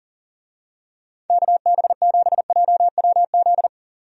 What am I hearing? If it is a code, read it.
Morse code, heard as a keyed tone, about 40 wpm, 700 Hz. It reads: KB7JWZ
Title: USA Call Signs - 6 Characters - 40wpm